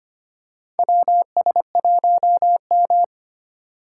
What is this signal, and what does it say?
Morse code, heard as a keyed tone, about 25 wpm, 700 Hz. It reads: WS1M
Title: USA Call Signs - 2x Character Spacing - 25wpm